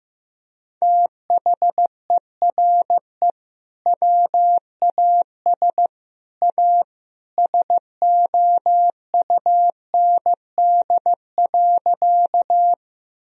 Morse code, heard as a keyed tone, about 15 wpm, 700 Hz. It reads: THERE WAS A SOUND.
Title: Sentences from Top 1000 Words - Review Part 1 - 15wpm